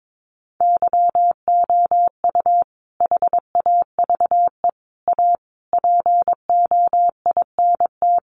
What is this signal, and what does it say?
Morse code, heard as a keyed tone, about 22 wpm, 700 Hz. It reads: YOU HAVE A POINT
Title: Sentences from Top 1400 Words 22wpm